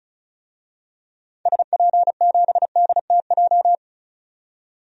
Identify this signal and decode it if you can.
Morse code, heard as a keyed tone, about 34 wpm, 700 Hz. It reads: SP7DTJ
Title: DXCC-100 Call Signs - 6 Characters - 35wpm